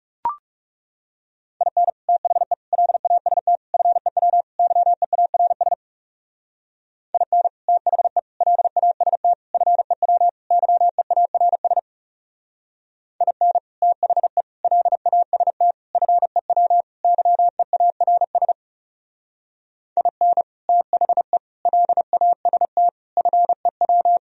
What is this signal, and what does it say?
Morse code, heard as a keyed tone, about 39 wpm, 700 Hz. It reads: IN THE LAST FEW YEARS IN THE LAST FEW YEARS IN THE LAST FEW YEARS IN THE LAST FEW